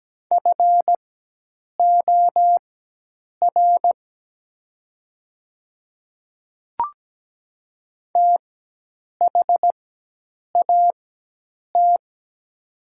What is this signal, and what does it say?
Morse code, heard as a keyed tone, about 17 wpm, spaced as for 8 wpm, 700 Hz. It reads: FOR THAT
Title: 4x Character Spacing - Top 100 Words - 17wpm